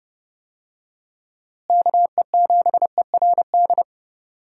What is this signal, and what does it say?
Morse code, heard as a keyed tone, about 30 wpm, 700 Hz. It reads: KE7ERD
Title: USA Call Signs - 6 Characters - Repeated 2x - 30wpm